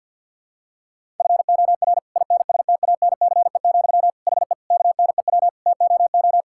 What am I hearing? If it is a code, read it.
Morse code, heard as a keyed tone, about 50 wpm, 700 Hz. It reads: FOR INSTANCE, HE KNEW TOO
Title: Sentences from Top 1000 Words - Review Part 5 - 50wpm